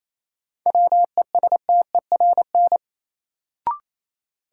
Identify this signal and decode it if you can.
Morse code, heard as a keyed tone, about 28 wpm, 700 Hz. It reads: WESTERN
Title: Sentences from Top 1000 Words - Review Part 7 - New Word Drill - 28wpm